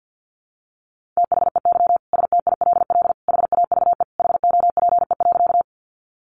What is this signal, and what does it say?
Morse code, heard as a keyed tone, about 50 wpm, 700 Hz. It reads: THEY STILL HAVE HOPE.
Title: Sentences from Top 1000 Words 50wpm